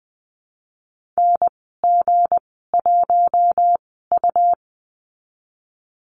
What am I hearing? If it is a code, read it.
Morse code, heard as a keyed tone, about 19 wpm, spaced as for 14 wpm, 700 Hz. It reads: NG1U
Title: 161 Lesson 26g - Letter g - Call Signs 20wpm